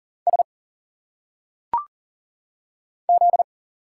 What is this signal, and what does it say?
Morse code, heard as a keyed tone, about 39 wpm, 700 Hz. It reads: S Z
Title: Single Letters - Phonetic Pronunciation 40wpm